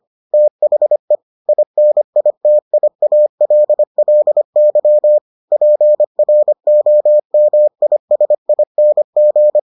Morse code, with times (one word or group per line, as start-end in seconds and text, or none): 0.3-1.1 THE
1.5-5.2 INITIALLY
5.5-9.6 PROMISING